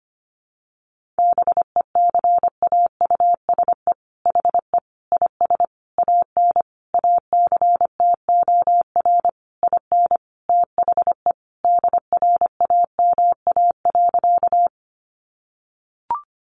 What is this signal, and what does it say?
Morse code, heard as a keyed tone, about 25 wpm, 700 Hz. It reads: BECAUSE HE IS AN ACTOR IN THE DRAMA.
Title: Sentences from Top 1500 Words 25wpm